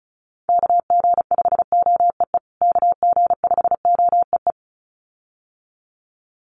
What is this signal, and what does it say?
Morse code, heard as a keyed tone, about 34 wpm, 700 Hz. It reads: KG5OEE KG5OEE
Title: USA Call Signs Repeated 2x 35wpm